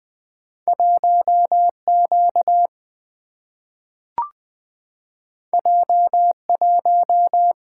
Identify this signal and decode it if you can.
Morse code, heard as a keyed tone, about 20 wpm, 700 Hz. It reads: 1 Q J 1
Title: Two Letter-Number Combinations - 20wpm